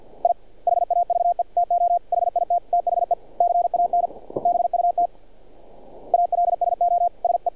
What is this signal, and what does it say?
Morse code, heard as a keyed tone, about 46 wpm, 700 Hz. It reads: I HAVE TO HIT THE CAN BUT ALSO SE
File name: archived / CRSnet